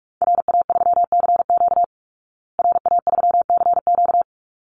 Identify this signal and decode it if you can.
Morse code, heard as a keyed tone, about 44 wpm, 700 Hz. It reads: RA3CQ RA3CQ
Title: Call Signs Repeated 3x - 45wpm